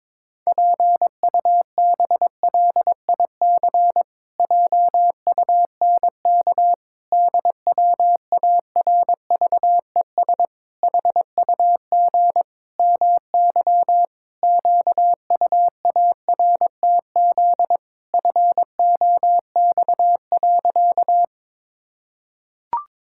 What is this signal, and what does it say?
Morse code, heard as a keyed tone, about 22 wpm, 700 Hz. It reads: PUBLIC JUNK DWARVES HUG MY QUARTZ FOX.
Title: Pangram Challenge v2 - 22wpm